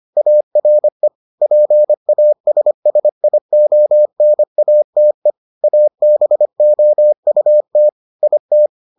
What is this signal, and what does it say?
Morse code, heard as a keyed tone, about 25 wpm, 600 Hz. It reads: ARE PASSIONATE ABOUT IT